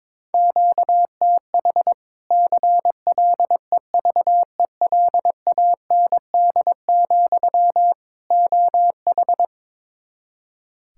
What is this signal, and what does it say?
Morse code, heard as a keyed tone, about 21 wpm, 700 Hz. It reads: QTH CLEVELAND, OH
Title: Practice QSO - Exchange 1 - Beginning - 22wpm